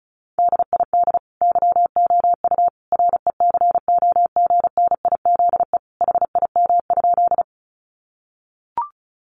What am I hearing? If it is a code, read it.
Morse code, heard as a keyed tone, about 34 wpm, 700 Hz. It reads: DID YOU RECOGNIZE HIM?
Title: Sentences from Top 1400 Words 35wpm